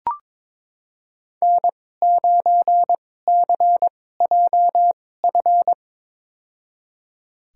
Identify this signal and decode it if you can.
Morse code, heard as a keyed tone, about 22 wpm, spaced as for 16 wpm, 700 Hz. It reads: N9CJF
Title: USA Call Signs - 5 Characters - 2x Character Spacing - 22wpm